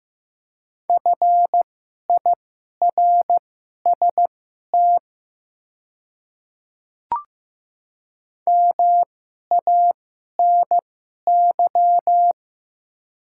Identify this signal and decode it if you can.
Morse code, heard as a keyed tone, about 15 wpm, spaced as for 11 wpm, 690 Hz. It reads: FIRST MANY
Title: 2x Character Spacing - Top 100 Words - 15wpm